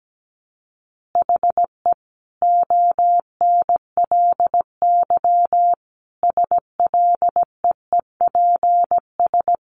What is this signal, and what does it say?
Morse code, heard as a keyed tone, about 17 wpm, 700 Hz. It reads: HE ONLY SLEEPS
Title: Additional Easy Phrases 17wpm